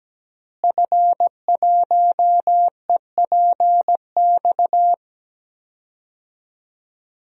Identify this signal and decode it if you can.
Morse code, heard as a keyed tone, about 17 wpm, 700 Hz. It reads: F1EPX
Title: DXCC-100 Call Signs - 5 Characters - Repeated 2x - 17wpm